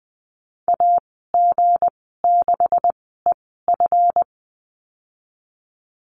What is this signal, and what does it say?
Morse code, heard as a keyed tone, about 20 wpm, spaced as for 15 wpm, 700 Hz. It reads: AG6EF
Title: 161 Lesson 26g - Letter g - Call Signs 20wpm